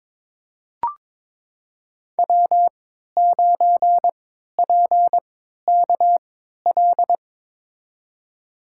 Morse code, heard as a keyed tone, about 22 wpm, spaced as for 12 wpm, 700 Hz. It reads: W9PKL
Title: USA Call Signs - 5 Characters - 3x Character Spacing - 22wpm